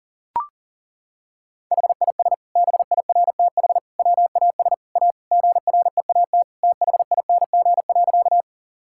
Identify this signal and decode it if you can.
Morse code, heard as a keyed tone, about 38 wpm, 700 Hz. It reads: HIS BIRTH WAS A GREAT THING.
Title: Sentences from Top 1000 Words - Review Part 5 - 40wpm